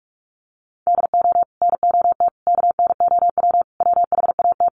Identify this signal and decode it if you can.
Morse code, heard as a keyed tone, about 44 wpm, 690 Hz. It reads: DO NOT KNOW WHAT
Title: Sets of 5 Words Encore2 45wpm